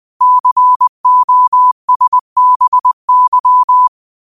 Morse code, heard as a keyed tone, about 20 wpm, 1000 Hz. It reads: COSBY